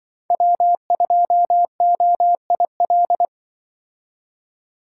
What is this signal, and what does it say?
Morse code, heard as a keyed tone, about 25 wpm, 700 Hz. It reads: W2OIL
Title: Speed Racing - USA 1x3 Call Signs 20wpm